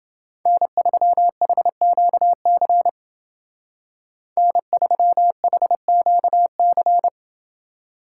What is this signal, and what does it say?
Morse code, heard as a keyed tone, about 28 wpm, 700 Hz. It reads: N3HQC N3HQC